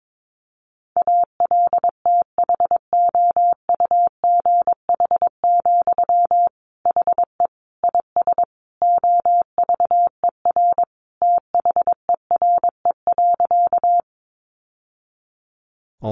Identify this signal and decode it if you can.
Morse code, heard as a keyed tone, about 22 wpm, 700 Hz. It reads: ALTHOUGH, HE IS OVER THERE.
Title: Sentences from Top 600 Words 22wpm